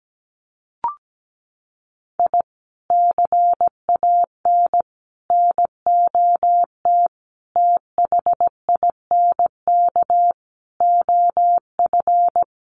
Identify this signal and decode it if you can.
Morse code, heard as a keyed tone, about 17 wpm, 700 Hz. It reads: I CAN NOT THINK OF
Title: Sets of 5 Words 17wpm